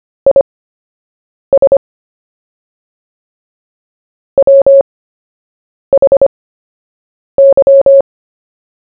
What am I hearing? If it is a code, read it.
Morse code, heard as a keyed tone, about 25 wpm, spaced as for 7 wpm, 550 Hz. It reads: IS WHY
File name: cw / fists / Short-Story-1-7wpm